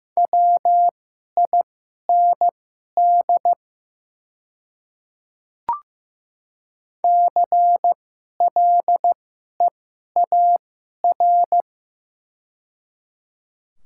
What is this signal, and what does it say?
Morse code, heard as a keyed tone, about 16 wpm, spaced as for 11 wpm, 690 Hz. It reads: WIND CLEAR
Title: 2x Character Spacing - Top 500 Words - 15wpm